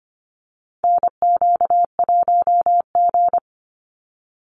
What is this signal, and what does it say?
Morse code, heard as a keyed tone, about 25 wpm, 700 Hz. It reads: NQ1G
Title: USA Call Signs 25wpm